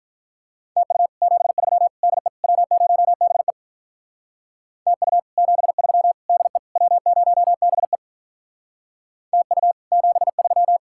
Morse code, heard as a keyed tone, about 44 wpm, 700 Hz. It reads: TU 73 DE W0BE TU 73 DE W0BE TU 73